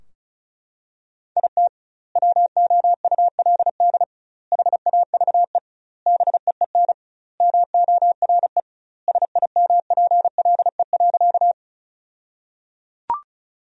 Morse code, heard as a keyed tone, about 36 wpm, 700 Hz. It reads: IT WOULD HAVE BEEN MORE SIMPLE.
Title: 2x Word-Spacing - Sentences from Top 400 Words - 35wpm